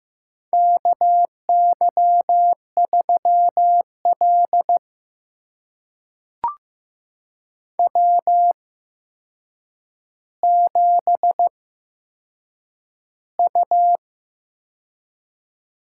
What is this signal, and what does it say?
Morse code, heard as a keyed tone, about 15 wpm, spaced as for 7 wpm, 700 Hz. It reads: KY3L W7U
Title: USA Call Signs - 4 Characters - 8x Character Spacing - 15wpm